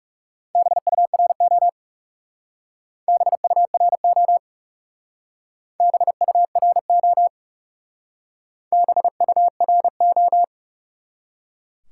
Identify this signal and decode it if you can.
Morse code, heard as a keyed tone, about 37 wpm, 700 Hz. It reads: BURO BURO BURO BURO
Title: Speed-Racing Top 90 QSO Elements 30wpm